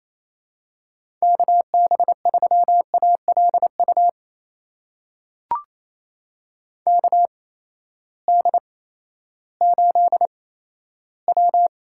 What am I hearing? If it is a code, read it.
Morse code, heard as a keyed tone, about 28 wpm, spaced as for 14 wpm, 700 Hz. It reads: KB3ALU KD8W
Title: USA Call Signs - 6 Characters - 8x Character Spacing - 28wpm